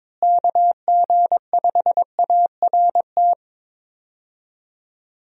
Morse code, heard as a keyed tone, about 22 wpm, 700 Hz. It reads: KG5ART
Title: USA Call Signs - 6 Characters - Repeated 2x - 22wpm